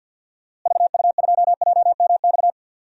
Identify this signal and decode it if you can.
Morse code, heard as a keyed tone, about 52 wpm, 700 Hz. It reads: VU2JMX
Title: DXCC-100 Call Signs - 6 Characters - Repeated 3x - 50wpm